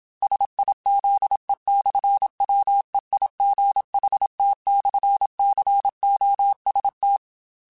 Morse code, heard as a keyed tone, about 27 wpm, 800 Hz. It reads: SIZE/WEIGHT/COST
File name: Technical Subjects / Technical Subjects-mp3s / Sw Pwr Sup @25